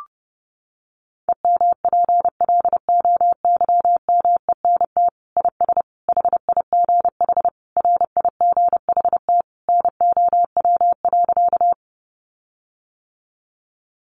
Morse code, heard as a keyed tone, about 30 wpm, 700 Hz. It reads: EMPLOYMENT IS HIGH RIGHT NOW.
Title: Sentences from Top 1500 Words 30wpm